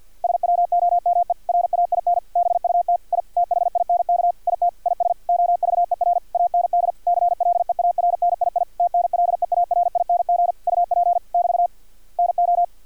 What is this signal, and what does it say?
Morse code, heard as a keyed tone, about 50 wpm, 700 Hz, with radio noise. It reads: SWOME RAIN BUT I THINK IT IS OVER AND CLEARNII TNLEARING UP <BT> NO
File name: archived / CRSnet